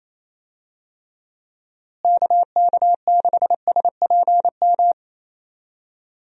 Morse code, heard as a keyed tone, about 28 wpm, 700 Hz. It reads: KK6SPM
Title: USA Call Signs - 6 Characters - Repeated 2x - 28wpm